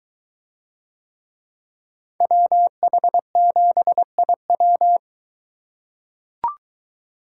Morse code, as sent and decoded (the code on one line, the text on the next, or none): .-- .... --... .. .--
WH7IW